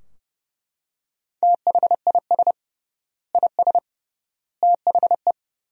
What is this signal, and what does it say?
Morse code, heard as a keyed tone, about 31 wpm, 700 Hz. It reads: THIS IS THE
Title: 3x Word-Spacing - Sentences from Top 200 Words - 30wpm